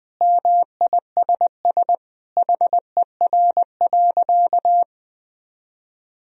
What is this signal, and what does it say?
Morse code, heard as a keyed tone, about 20 wpm, 700 Hz. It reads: MISS HER.
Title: Sentences from Top 1000 Words - Review Part 3 - 20wpm